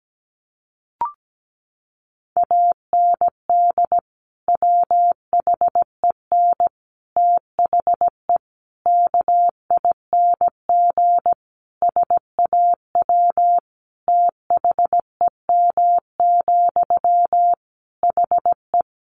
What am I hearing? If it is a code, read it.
Morse code, heard as a keyed tone, about 17 wpm, 700 Hz. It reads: AND WHEN THE KING SAW THEM, HE
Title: Sentences from Top 1100 Words 17wpm